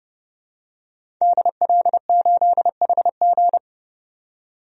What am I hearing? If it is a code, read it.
Morse code, heard as a keyed tone, about 32 wpm, 700 Hz. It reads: DL8HG
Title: DXCC-100 Call Signs - 5 Characters - 30wpm